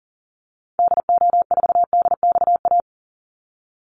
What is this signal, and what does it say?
Morse code, heard as a keyed tone, about 40 wpm, 700 Hz. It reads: DO4DXA
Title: Call Signs Repeated 3x - 40wpm